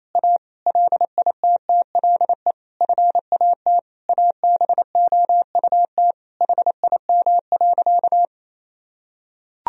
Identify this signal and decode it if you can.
Morse code, heard as a keyed tone, about 28 wpm, 700 Hz. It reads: A LITTLE FAT ABOUT HIM.
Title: Sentences from Top 1000 Words - Review Part 4 - 28wpm